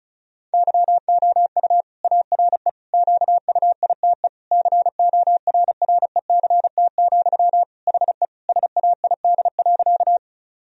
Morse code, heard as a keyed tone, about 34 wpm, 700 Hz. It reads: YOU ARE QUITE CORRECT, HE SAID.
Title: Sentences from Top 1000 Words - Review Part 2 - 35wpm